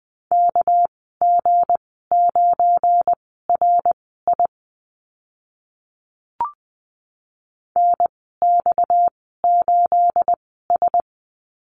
Morse code, heard as a keyed tone, about 20 wpm, spaced as for 14 wpm, 700 Hz. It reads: KG9RI NX8S
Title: 245 Lesson 39g - Letter z - Call Signs 20wpm